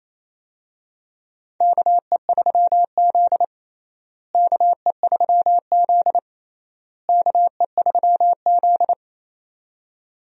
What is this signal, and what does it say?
Morse code, heard as a keyed tone, about 28 wpm, 700 Hz. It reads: KE3Z KE3Z KE3Z
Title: USA Call Signs - 4 Characters - Repeated 3x - 28wpm